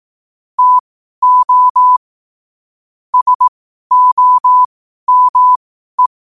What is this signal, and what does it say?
Morse code, heard as a keyed tone, about 18 wpm, spaced as for 12 wpm, 1000 Hz. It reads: TO SOME
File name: qotdfiles / 12wpm